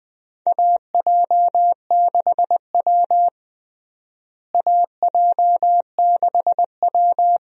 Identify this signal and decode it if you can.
Morse code, heard as a keyed tone, about 20 wpm, 700 Hz. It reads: AJ6W AJ6W
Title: USA Call Signs - 4 Characters - Repeated 2x - 20wpm